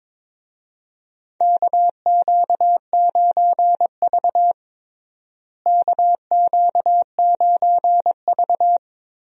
Morse code, heard as a keyed tone, about 22 wpm, 700 Hz. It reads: KQ9V KQ9V
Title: USA Call Signs - 4 Characters - Repeated 3x - 22wpm